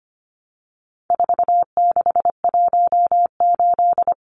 Morse code, heard as a keyed tone, about 25 wpm, 700 Hz. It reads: 4618
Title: Numbers - 4-Digit 25wpm